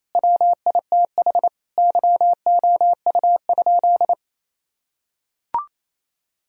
Morse code, heard as a keyed tone, about 27 wpm, 700 Hz. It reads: WITH YOU?
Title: Sentences from Top 1000 Words - Review Part 3 - 28wpm